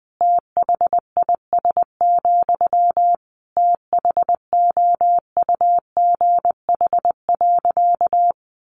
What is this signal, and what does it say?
Morse code, heard as a keyed tone, about 20 wpm, 700 Hz. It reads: THIS, THOUGH.
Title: Sentences from Top 1400 Words 20wpm